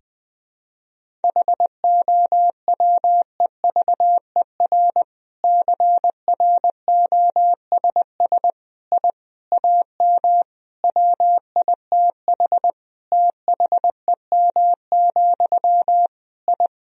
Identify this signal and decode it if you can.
Morse code, heard as a keyed tone, about 21 wpm, 700 Hz. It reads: HOWEVER CROSS I AM WITH THEM, I
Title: Sentences from Top 1000 Words - Review Part 1 - 20wpm